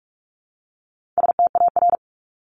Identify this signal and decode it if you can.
Morse code, heard as a keyed tone, about 44 wpm, 700 Hz. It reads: STAR
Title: Top 500 Words Repeated 45wpm